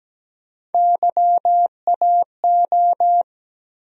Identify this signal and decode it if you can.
Morse code, heard as a keyed tone, about 17 wpm, 700 Hz. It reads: YAO
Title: USA Call Sign Postfixes - 3 Letters 17wpm